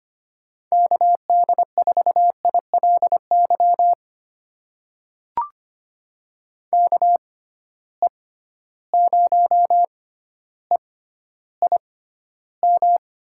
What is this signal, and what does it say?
Morse code, heard as a keyed tone, about 25 wpm, spaced as for 13 wpm, 700 Hz. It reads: KD4ILY KE0EIM
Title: USA Call Signs - 6 Characters - 6x Character Spacing - 25wpm